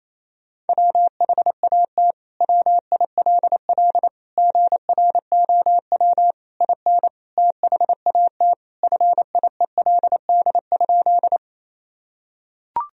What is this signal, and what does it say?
Morse code, heard as a keyed tone, about 28 wpm, 700 Hz. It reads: WHAT WILL GROW IN THAT FIELD?
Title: Sentences from Top 1000 Words - Review Part 2 - 28wpm